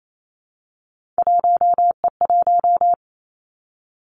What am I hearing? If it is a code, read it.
Morse code, heard as a keyed tone, about 28 wpm, 700 Hz. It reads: 1E1
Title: Three Challenging Letter-Number Combinations 28wpm